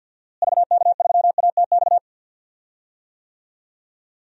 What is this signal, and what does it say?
Morse code, heard as a keyed tone, about 47 wpm, 700 Hz. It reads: VK3ATX
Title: DXCC-100 Call Signs - 6 Characters - Repeated 3x - 50wpm